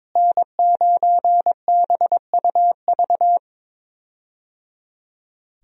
Morse code, heard as a keyed tone, about 22 wpm, 700 Hz. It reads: N9BUV
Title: USA Call Signs - 5 Characters - Repeated 3x - 22wpm